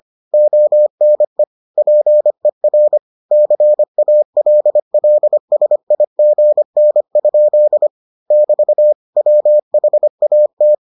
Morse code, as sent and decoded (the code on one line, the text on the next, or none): --- -. / .--. .-. / -.-. .- .-.. .-.. ... .. --. -. ..--.. / -...- / .-- .... .- -
ONE PER CALLSIGN? <BT> WHAT